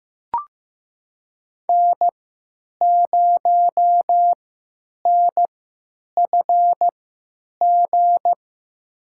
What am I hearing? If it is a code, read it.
Morse code, heard as a keyed tone, about 15 wpm, spaced as for 9 wpm, 700 Hz. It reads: N0NFG
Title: USA Call Signs - 5 Characters - 3x Character Spacing - 15wpm